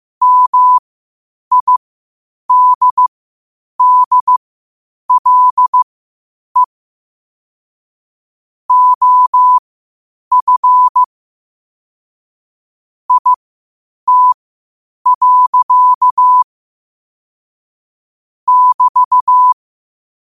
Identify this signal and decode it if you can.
Morse code, heard as a keyed tone, about 15 wpm, spaced as for 8 wpm, 1000 Hz. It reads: MIDDLE OF IT. <BT>
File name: qotdfiles / 8wpm